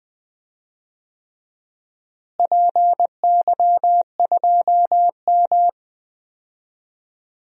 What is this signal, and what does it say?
Morse code, heard as a keyed tone, about 20 wpm, 700 Hz. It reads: PY2M
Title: DXCC-100 Call Signs - 4 Characters - Repeated 3x - 20wpm